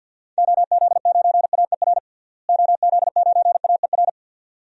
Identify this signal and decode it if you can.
Morse code, heard as a keyed tone, about 50 wpm, 700 Hz. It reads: OZ9AER OZ9AER
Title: DXCC-100 Call Signs - 6 Characters - Repeated 2x - 50wpm